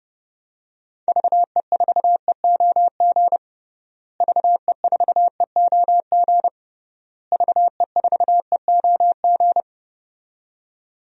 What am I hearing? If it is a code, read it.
Morse code, heard as a keyed tone, about 30 wpm, 700 Hz. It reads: VE4EOG VE4EOG VE4EOG
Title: DXCC-100 Call Signs - 6 Characters - Repeated 3x - 30wpm